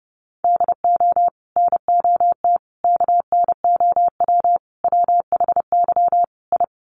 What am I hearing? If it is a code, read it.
Morse code, heard as a keyed tone, about 30 wpm, 700 Hz. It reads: DO NOT KNOW WHY I
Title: Sets of 5 Words Encore 30wpm